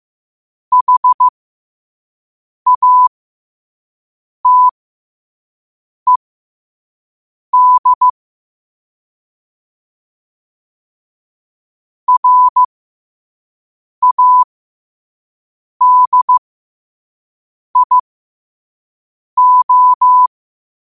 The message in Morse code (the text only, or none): HATED RADIO